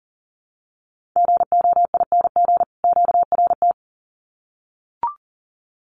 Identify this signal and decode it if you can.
Morse code, heard as a keyed tone, about 40 wpm, 700 Hz. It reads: GOING QRT
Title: Common QSO Phrases 40wpm